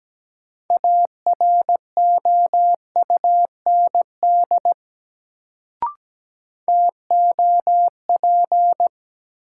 Morse code, heard as a keyed tone, about 17 wpm, 700 Hz. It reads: AROUND TOP
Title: Sets of 2 Words - New Word Drill - 17wpm